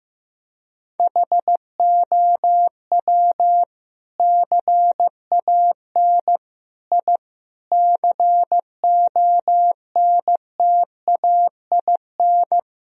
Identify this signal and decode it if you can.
Morse code, heard as a keyed tone, about 15 wpm, 700 Hz. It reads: HOW CAN I CONTAIN
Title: Sentences from Top 1000 Words - Review Part 2 - 15wpm